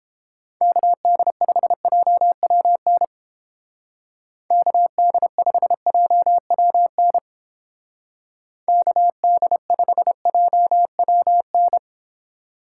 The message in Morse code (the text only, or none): KD5JWN KD5JWN KD5JWN